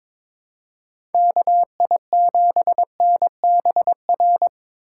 Morse code, heard as a keyed tone, about 22 wpm, 700 Hz. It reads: KI7NBR
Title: USA Call Signs - 4x Character Spacing - 22wpm